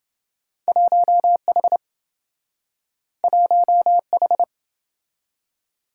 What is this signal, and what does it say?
Morse code, heard as a keyed tone, about 27 wpm, 700 Hz. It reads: 1H 1H